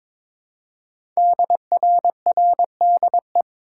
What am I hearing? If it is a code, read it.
Morse code, heard as a keyed tone, about 22 wpm, 700 Hz. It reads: DRRDE
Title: Code Groups - 5 Letters - 22wpm